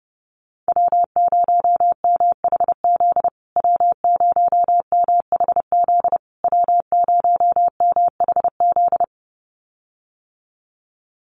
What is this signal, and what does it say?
Morse code, heard as a keyed tone, about 30 wpm, 700 Hz. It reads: W0MHZ W0MHZ W0MHZ
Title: USA Call Signs Repeated 3x 30wpm